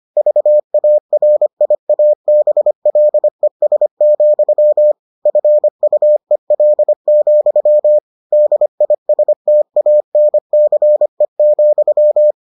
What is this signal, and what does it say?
Morse code, heard as a keyed tone, about 25 wpm, 600 Hz. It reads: VARIABLES, FUEL, DISTANCE,